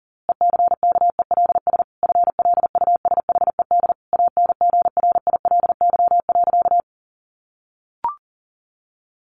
Morse code, heard as a keyed tone, about 40 wpm, 700 Hz. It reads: ECKELS FLUSHED ANGRILY.
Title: A Sound Of Thunder 40wpm